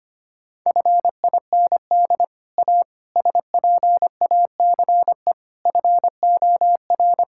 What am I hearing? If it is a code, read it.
Morse code, heard as a keyed tone, about 25 wpm, 700 Hz. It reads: FIND A SPACE FOR
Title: Sentences from Top 1000 Words - Review Part 5 - 25wpm